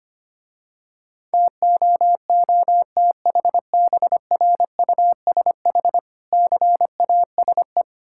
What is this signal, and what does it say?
Morse code, heard as a keyed tone, about 25 wpm, 700 Hz. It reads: TOOTHBRUSH CASE
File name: Easy Words - Speed-Racing with Hints - 25wpm